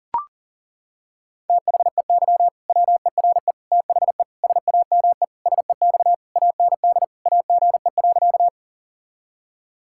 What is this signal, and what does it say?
Morse code, heard as a keyed tone, about 40 wpm, 700 Hz. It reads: THEY WERE THE SAME SEX AND AGE.
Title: Sentences from Top 1600 Words 40wpm